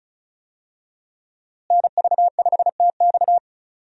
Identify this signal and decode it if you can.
Morse code, heard as a keyed tone, about 36 wpm, 700 Hz. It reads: NV5TX
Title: USA Call Signs - 5 Characters - Repeated 2x - 35wpm